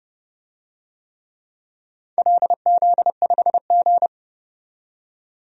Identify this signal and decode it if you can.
Morse code, heard as a keyed tone, about 32 wpm, 700 Hz. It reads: LZ5G
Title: DXCC-100 Call Signs - 4 Characters - 30wpm